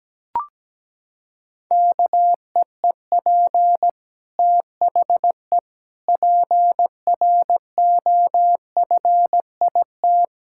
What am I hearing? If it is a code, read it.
Morse code, heard as a keyed tone, about 17 wpm, 700 Hz. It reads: KEEP THE PROFIT
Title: Sentences from Top 1000 Words - Review Part 5 - 17wpm